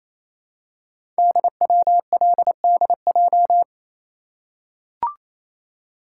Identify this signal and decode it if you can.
Morse code, heard as a keyed tone, about 27 wpm, 700 Hz. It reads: DWLDJ
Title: Code Groups - 5 Letters - 28wpm